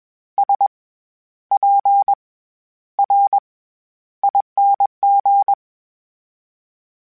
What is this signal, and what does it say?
Morse code, heard as a keyed tone, about 21 wpm, spaced as for 11 wpm, 800 Hz. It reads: SPRING
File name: Special INGs-mp3s / Special INGs-mp3s / SlowToFaster ing @20